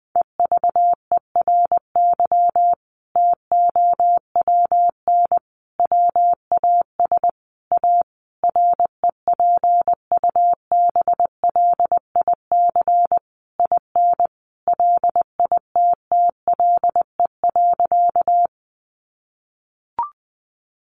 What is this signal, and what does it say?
Morse code, heard as a keyed tone, about 20 wpm, 700 Hz. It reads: EVERY TOWN WAS A REPUBLIC IN LITTLE.
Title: Sentences from Top 1500 Words 20wpm